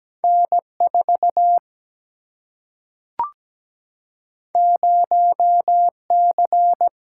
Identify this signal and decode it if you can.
Morse code, heard as a keyed tone, about 17 wpm, 700 Hz. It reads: N4 0C
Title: Single-Letter Single-Number 17wpm